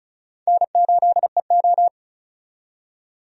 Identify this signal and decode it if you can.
Morse code, heard as a keyed tone, about 36 wpm, 700 Hz. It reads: N8EO
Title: USA Call Signs - 4 Characters - 35wpm